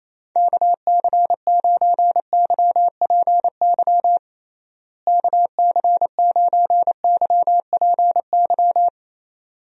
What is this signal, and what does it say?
Morse code, heard as a keyed tone, about 27 wpm, 700 Hz. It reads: KC9YPY KC9YPY
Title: USA Call Signs - 6 Characters - Repeated 2x - 28wpm